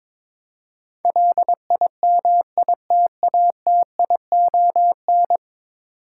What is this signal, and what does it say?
Morse code, heard as a keyed tone, about 22 wpm, 700 Hz. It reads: LIMITATION
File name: practice / 22wpm / 10 Letter Words - First 3 Letters Given - 22wpm